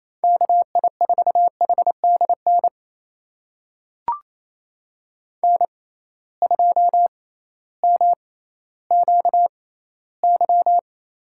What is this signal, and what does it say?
Morse code, heard as a keyed tone, about 28 wpm, spaced as for 15 wpm, 700 Hz. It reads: KI4HDN N2MQY
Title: USA Call Signs - 6x Character Spacing - 28wpm